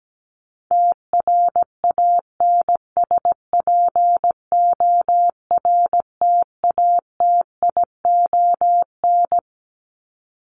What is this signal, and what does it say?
Morse code, heard as a keyed tone, about 17 wpm, 700 Hz. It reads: TRANSPORTATION
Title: Top 1501-1750 Words 17wpm